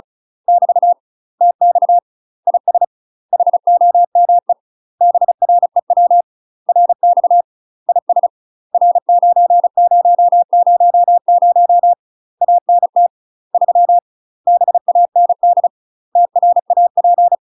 Morse code, heard as a keyed tone, about 34 wpm, 700 Hz. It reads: <BT> TX IS HOME BREW RX IS R9000 ANT 3 BAND TRAP
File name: UK MASTER QSOs TRAINING / 35WPM / UK MASTER QSOs 0002